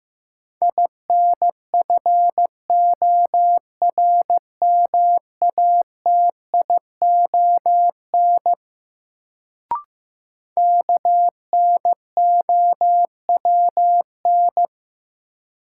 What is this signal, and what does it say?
Morse code, heard as a keyed tone, about 15 wpm, 700 Hz. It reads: INFORMATION KNOWN
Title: Sentences from Top 600 Words - New Word Drill - 15wpm